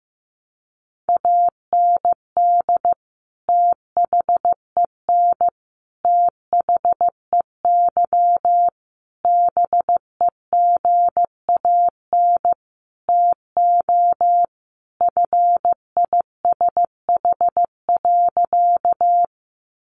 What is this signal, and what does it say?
Morse code, heard as a keyed tone, about 15 wpm, 700 Hz. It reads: AND THEN THEY BEGAN TO FISH.
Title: Sentences from Top 300 Words 15wpm